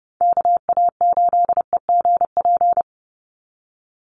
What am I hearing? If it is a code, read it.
Morse code, heard as a keyed tone, about 30 wpm, 700 Hz. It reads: KA8EGP
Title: USA Call Signs 30wpm